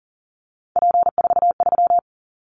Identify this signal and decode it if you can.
Morse code, heard as a keyed tone, about 40 wpm, 700 Hz. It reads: P43
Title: Callsign Prefixes for Top 100 DXCC - 40wpm